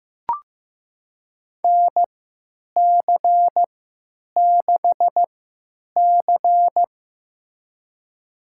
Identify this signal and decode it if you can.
Morse code, heard as a keyed tone, about 15 wpm, spaced as for 9 wpm, 700 Hz. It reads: NC6C
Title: USA Call Signs - 4 Characters - 3x Character Spacing - 15wpm